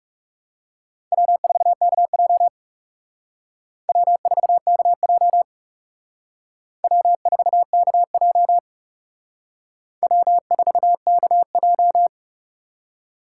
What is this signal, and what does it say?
Morse code, heard as a keyed tone, about 37 wpm, 700 Hz. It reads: W4KJ W4KJ W4KJ W4KJ
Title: Speed Racing - USA 1x2 Call Signs 30wpm